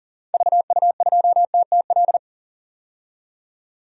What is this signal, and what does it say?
Morse code, heard as a keyed tone, about 40 wpm, 700 Hz. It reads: VU2TTL
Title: DXCC-100 Call Signs - 6 Characters - 40wpm